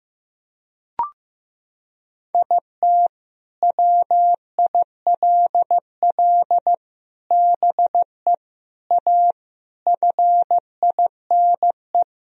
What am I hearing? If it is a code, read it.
Morse code, heard as a keyed tone, about 15 wpm, 700 Hz. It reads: IT WILL BE A FINE
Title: Sentences from Top 1000 Words - Review Part 2 - 15wpm